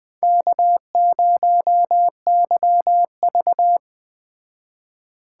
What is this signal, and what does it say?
Morse code, heard as a keyed tone, about 20 wpm, 700 Hz. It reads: K0YV